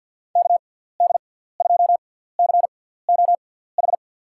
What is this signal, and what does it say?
Morse code, heard as a keyed tone, about 53 wpm, spaced as for 17 wpm, 700 Hz. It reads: KD2COH
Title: USA Call Signs - 6 Characters - 6x Character Spacing - 50wpm